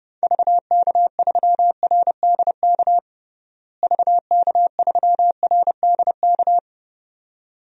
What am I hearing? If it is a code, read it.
Morse code, heard as a keyed tone, about 32 wpm, 700 Hz. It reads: VK3RDK VK3RDK
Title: DXCC-100 Call Signs - 6 Characters - Repeated 2x - 30wpm